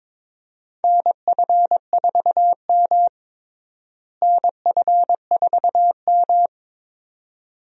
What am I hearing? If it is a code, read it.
Morse code, heard as a keyed tone, about 22 wpm, 700 Hz. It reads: NF4M NF4M
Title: USA Call Signs - 4 Characters - Repeated 2x - 22wpm